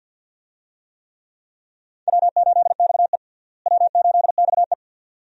Morse code, heard as a keyed tone, about 48 wpm, 700 Hz. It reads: W8XE W8XE
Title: USA Call Signs - 4 Characters - Repeated 2x - 50wpm